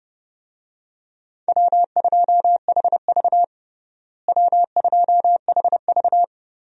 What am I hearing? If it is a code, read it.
Morse code, heard as a keyed tone, about 30 wpm, 700 Hz. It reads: W2HV W2HV
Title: USA Call Signs - 4 Characters - Repeated 3x - 30wpm